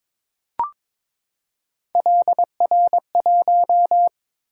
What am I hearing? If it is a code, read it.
Morse code, heard as a keyed tone, about 22 wpm, 700 Hz. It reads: LR1
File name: practice / 22wpm / Three Letter-Number Combinations - 22wpm